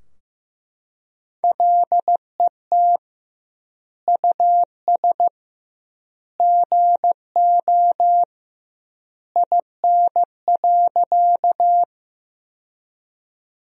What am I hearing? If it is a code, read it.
Morse code, heard as a keyed tone, about 15 wpm, 700 Hz. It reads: LET US GO IN.
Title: 2x Word-Spacing - Sentences from Top 300 Words - 15wpm